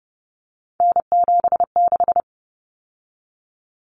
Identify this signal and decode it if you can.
Morse code, heard as a keyed tone, about 30 wpm, 700 Hz. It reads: N76
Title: Common Cut Numbers - 3-Digit 30wpm